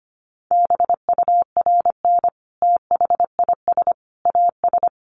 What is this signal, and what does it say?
Morse code, heard as a keyed tone, about 25 wpm, 700 Hz. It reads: BURN THIS AS
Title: Sentences from Top 1300 Words 25wpm